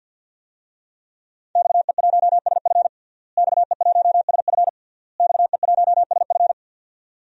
Morse code, heard as a keyed tone, about 50 wpm, 700 Hz. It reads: XE1SF XE1SF XE1SF
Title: DXCC-100 Call Signs - 5 Characters - Repeated 3x - 50wpm